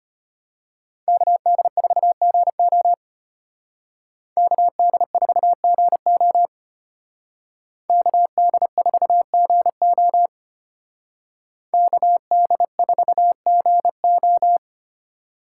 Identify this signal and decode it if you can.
Morse code, heard as a keyed tone, about 33 wpm, 700 Hz. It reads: KD4GO KD4GO KD4GO KD4GO